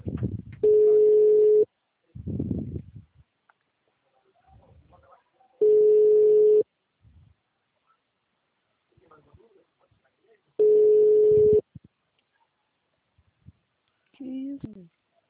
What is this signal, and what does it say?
Ringback tone (Europe, 425 Hz), with speech.
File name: out- -2004- - - .61393